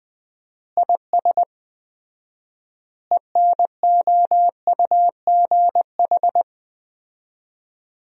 Morse code, heard as a keyed tone, about 20 wpm, 700 Hz. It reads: IS ENOUGH
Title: 4x Word-Spacing - Sentences from Top 400 Words - 20wpm